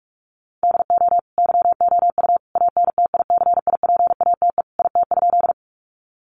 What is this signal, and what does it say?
Morse code, heard as a keyed tone, about 44 wpm, 700 Hz. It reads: DO YOU ANTICIPATE IT?
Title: Sentences from Top 1500 Words 45wpm